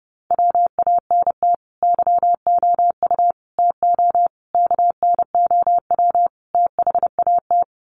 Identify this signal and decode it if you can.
Morse code, heard as a keyed tone, about 30 wpm, 700 Hz. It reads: WANT YOU TO KNOW THAT
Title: Sets of 5 Words Encore 30wpm